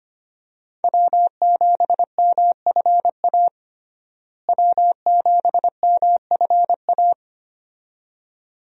Morse code, heard as a keyed tone, about 25 wpm, 700 Hz. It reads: W7MFA W7MFA
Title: USA Call Signs - 5 Characters - Repeated 2x - 25wpm